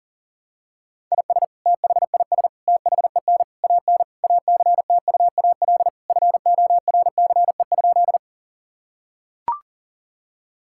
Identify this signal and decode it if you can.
Morse code, heard as a keyed tone, about 40 wpm, 700 Hz. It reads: IS THIS THEN AN ACTUAL FORCE?
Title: Sentences from Top 1000 Words - Review Part 5 - 40wpm